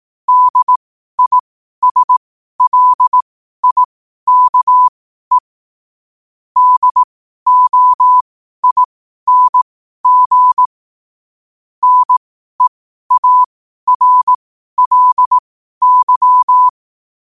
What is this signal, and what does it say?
Morse code, heard as a keyed tone, about 18 wpm, spaced as for 12 wpm, 1000 Hz. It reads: DISLIKE DOING NEARLY
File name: qotdfiles / 12wpm